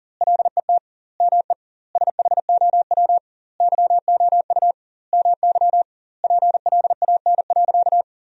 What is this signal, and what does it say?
Morse code, heard as a keyed tone, about 37 wpm, 700 Hz. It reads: LET ME SHOW YOU MY PLAN.
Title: 2x Word-Spacing - Sentences from Top 500 Words - 40wpm